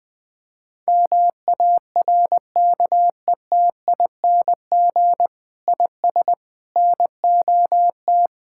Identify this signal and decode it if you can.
Morse code, heard as a keyed tone, about 20 wpm, 700 Hz. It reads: MARKETING IS NOT
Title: Sentences from Top 1000 Words - Review Part 5 - 20wpm